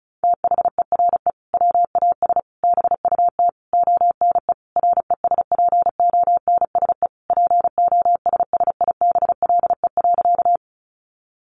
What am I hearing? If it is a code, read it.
Morse code, heard as a keyed tone, about 34 wpm, 700 Hz. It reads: THERE WAS BUT ONE RESPONSE POSSIBLE.
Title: Sentences from Top 1400 Words 35wpm